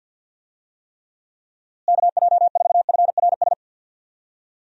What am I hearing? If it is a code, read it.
Morse code, heard as a keyed tone, about 48 wpm, 700 Hz. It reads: KJ4FRS
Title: USA Call Signs - 6 Characters - Repeated 3x - 50wpm